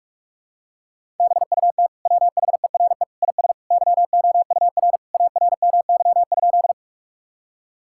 Morse code, heard as a keyed tone, about 46 wpm, 700 Hz. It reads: BUT WHERE IS YOUR ARMY?
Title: Sentences from Top 1000 Words - Review Part 5 - 45wpm